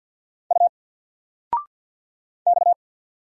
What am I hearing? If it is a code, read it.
Morse code, heard as a keyed tone, about 50 wpm, 700 Hz. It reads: U X
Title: Single Letter-Number 50wpm